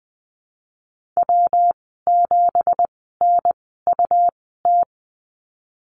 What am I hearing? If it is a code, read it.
Morse code, heard as a keyed tone, about 20 wpm, spaced as for 15 wpm, 700 Hz. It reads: W7NUT
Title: 186 Lesson 30g - Letter b - Call Signs 20wpm